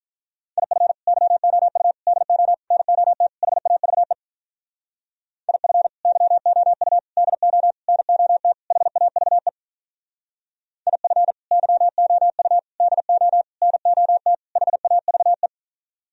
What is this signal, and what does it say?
Morse code, heard as a keyed tone, about 45 wpm, 700 Hz. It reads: IF YOU DO NOT HAVE IF YOU DO NOT HAVE IF YOU DO NOT HAVE